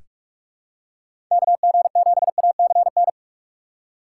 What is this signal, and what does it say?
Morse code, heard as a keyed tone, about 44 wpm, 700 Hz. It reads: KG7ACN
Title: USA Call Signs - 6 Characters - 4x Character Spacing - 45wpm